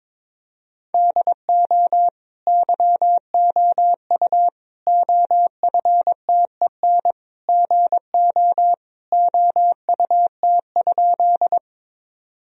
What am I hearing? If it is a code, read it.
Morse code, heard as a keyed tone, about 22 wpm, 700 Hz. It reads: DO YOU OFTEN GO OUT?
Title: Sentences from Top 1000 Words - Review Part 1 - 22wpm